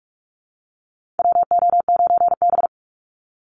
Morse code, heard as a keyed tone, about 44 wpm, 700 Hz. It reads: WO9B
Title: Call Signs Repeated 3x - 45wpm